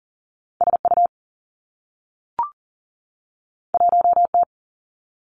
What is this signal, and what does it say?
Morse code, heard as a keyed tone, about 39 wpm, 700 Hz. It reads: SU 1T
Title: Two Challenging Letter-Number Combinations 40wpm